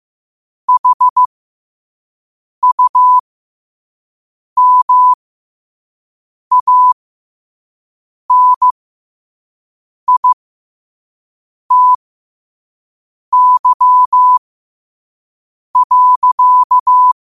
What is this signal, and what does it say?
Morse code, heard as a keyed tone, about 15 wpm, spaced as for 5 wpm, 1000 Hz. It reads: HUMANITY.
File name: qotdfiles / 5wpm